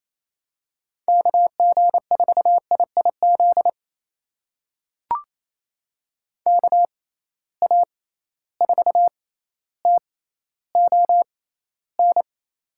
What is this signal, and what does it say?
Morse code, heard as a keyed tone, about 28 wpm, spaced as for 14 wpm, 700 Hz. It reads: KG4IIZ KA4TON
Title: USA Call Signs - 6 Characters - 6x Character Spacing - 28wpm